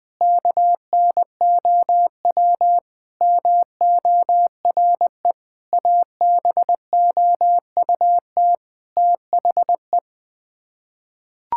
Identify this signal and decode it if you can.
Morse code, heard as a keyed tone, about 21 wpm, 700 Hz. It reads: KNOW MORE ABOUT THE
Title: Sets of 8 Words - 20wpm